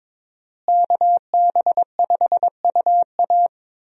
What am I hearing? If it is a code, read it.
Morse code, heard as a keyed tone, about 22 wpm, 700 Hz. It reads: KB5UA
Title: USA Call Signs - 5 Characters - Repeated 3x - 22wpm